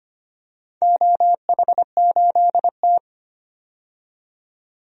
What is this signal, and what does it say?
Morse code, heard as a keyed tone, about 25 wpm, 700 Hz. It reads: OH8T
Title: DXCC-100 Call Signs - 4 Characters - 25wpm